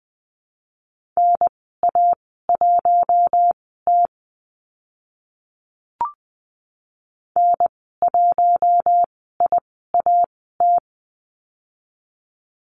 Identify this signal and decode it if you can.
Morse code, heard as a keyed tone, about 20 wpm, spaced as for 15 wpm, 700 Hz. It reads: NA1T N1IAT
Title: 042 Lesson 8g - Number 1 - Call Signs 20wpm